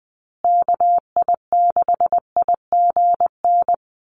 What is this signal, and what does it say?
Morse code, heard as a keyed tone, about 20 wpm, 700 Hz. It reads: KI6IGN
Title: USA Call Signs Repeated 2x 20wpm